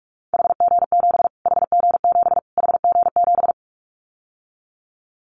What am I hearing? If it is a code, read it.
Morse code, heard as a keyed tone, about 44 wpm, 700 Hz. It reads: HG7 HG7 HG7
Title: Callsign Prefixes for Top 25 DXCC - Repeated 3x - 45wpm